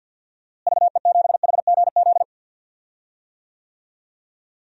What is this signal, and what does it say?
Morse code, heard as a keyed tone, about 53 wpm, 700 Hz. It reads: VE7HGZ
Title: DXCC-100 Call Signs - 6 Characters - Repeated 2x - 50wpm